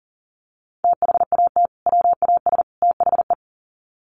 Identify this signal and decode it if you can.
Morse code, heard as a keyed tone, about 40 wpm, 700 Hz. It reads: THAT WAS THE
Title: Sets of 3 Words 40wpm